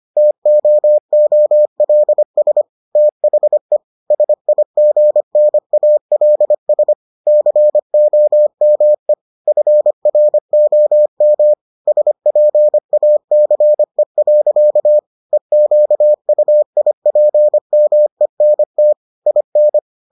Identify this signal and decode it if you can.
Morse code, heard as a keyed tone, about 25 wpm, 600 Hz. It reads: TOOLS THE SIGNALS COME FROM SPACE. EQUIPMENT IN